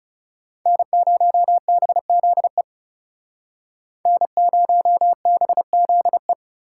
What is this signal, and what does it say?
Morse code, heard as a keyed tone, about 37 wpm, 700 Hz. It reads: N0BZE N0BZE
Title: Speed Racing - USA 1x3 Call Signs 30wpm